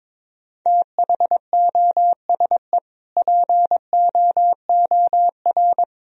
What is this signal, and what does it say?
Morse code, heard as a keyed tone, about 22 wpm, 700 Hz. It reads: THOSE POOR
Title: Sentences from Top 1000 Words - Review Part 1 - 22wpm